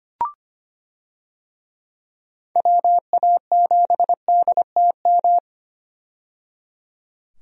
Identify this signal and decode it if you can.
Morse code, heard as a keyed tone, about 25 wpm, 700 Hz. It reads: WA7DTM
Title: USA Call Signs - 6 Characters - 25wpm